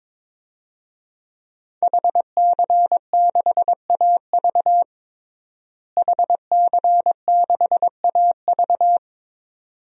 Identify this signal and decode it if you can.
Morse code, heard as a keyed tone, about 22 wpm, 700 Hz. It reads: HC6AV HC6AV
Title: DXCC-100 Call Signs - 5 Characters - Repeated 2x - 22wpm